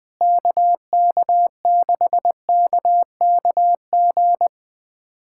Morse code, heard as a keyed tone, about 21 wpm, 700 Hz. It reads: KK6KKG
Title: USA Callsign DE Callsign - 6 Characters - 20wpm